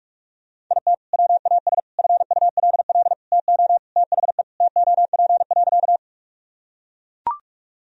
Morse code, heard as a keyed tone, about 42 wpm, 700 Hz. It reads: IT WAS FULL TO THE TOP.
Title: Sentences from Top 1000 Words - Review Part 3 - 45wpm